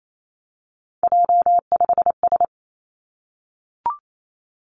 Morse code, heard as a keyed tone, about 28 wpm, 700 Hz. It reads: J5S
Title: Three Challenging Letter-Number Combinations 28wpm